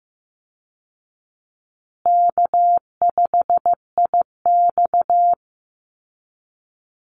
Morse code, heard as a keyed tone, about 15 wpm, 700 Hz. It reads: K5IX
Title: Easy Call Signs Repeated 2x - 15wpm